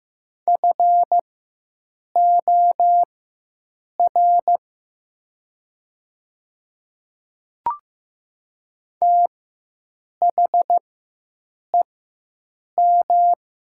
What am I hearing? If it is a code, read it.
Morse code, heard as a keyed tone, about 15 wpm, spaced as for 7 wpm, 700 Hz. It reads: FOR THEM
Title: 4x Character Spacing - Top 100 Words - 15wpm